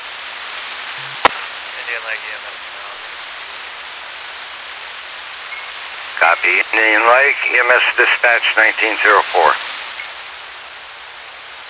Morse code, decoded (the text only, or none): none